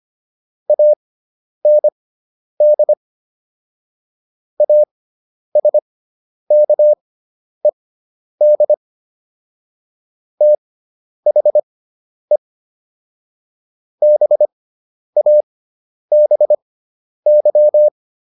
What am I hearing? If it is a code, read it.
Morse code, heard as a keyed tone, about 25 wpm, spaced as for 10 wpm, 600 Hz. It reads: AND ASKED THE BABY